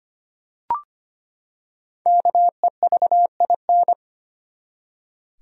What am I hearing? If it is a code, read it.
Morse code, heard as a keyed tone, about 25 wpm, 700 Hz. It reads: KEVIN
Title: USA Common Names - Max 5 Letters - 25wpm